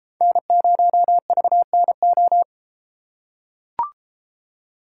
Morse code, heard as a keyed tone, about 32 wpm, 700 Hz. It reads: N0VNO